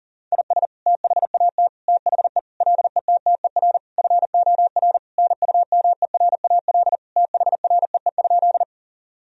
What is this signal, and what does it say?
Morse code, heard as a keyed tone, about 40 wpm, 700 Hz. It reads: IS THAT THE LETTER FOR NUMERAL THREE?
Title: Sentences from Top 1000 Words - Review Part 2 - 40wpm